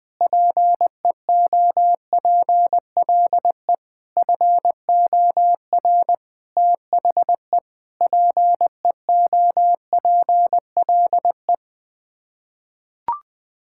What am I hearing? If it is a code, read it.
Morse code, heard as a keyed tone, about 20 wpm, 700 Hz. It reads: PEOPLE FOR THE PEOPLE
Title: Sets of 9 Words - 20wpm